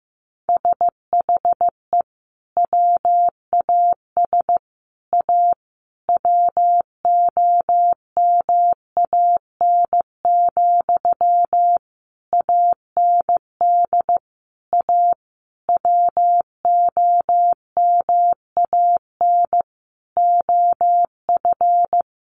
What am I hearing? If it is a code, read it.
Morse code, heard as a keyed tone, about 15 wpm, 700 Hz. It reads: SHE WAS A WOMAN, AND A WOMAN OF
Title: Sentences from Top 1400 Words 15wpm